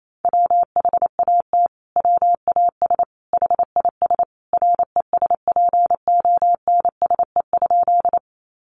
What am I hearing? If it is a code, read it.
Morse code, heard as a keyed tone, about 28 wpm, 700 Hz. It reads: WHAT WAS HIS RESPONSE?
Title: Sentences from Top 1400 Words 28wpm